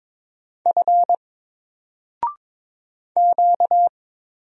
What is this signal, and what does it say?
Morse code, heard as a keyed tone, about 22 wpm, 700 Hz. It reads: F Q
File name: Single-Letter-Number-22wpm